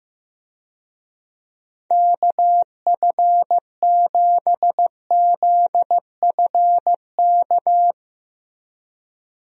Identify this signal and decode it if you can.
Morse code, heard as a keyed tone, about 15 wpm, 700 Hz. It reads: KF7ZFK
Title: USA Call Signs - 6 Characters - Repeated 3x - 15wpm